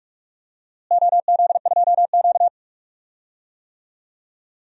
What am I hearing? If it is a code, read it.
Morse code, heard as a keyed tone, about 46 wpm, 700 Hz. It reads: OZ2Q
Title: DXCC-100 Call Signs - 4 Characters - Repeated 3x - 45wpm